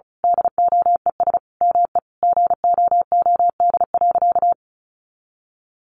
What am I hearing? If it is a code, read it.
Morse code, heard as a keyed tone, about 34 wpm, 700 Hz. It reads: DOES ME GOOD.
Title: Sentences from Top 200 Words 35wpm